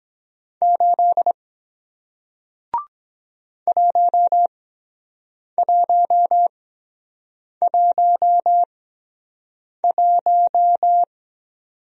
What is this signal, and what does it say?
Morse code, heard as a keyed tone, about 22 wpm, spaced as for 7 wpm, 700 Hz. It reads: 8 1111